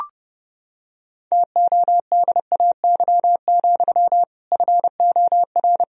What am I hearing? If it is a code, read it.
Morse code, heard as a keyed tone, about 32 wpm, 700 Hz. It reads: TODAY, FOR
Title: Sentences from Top 1000 Words - Review Part 1 - 30wpm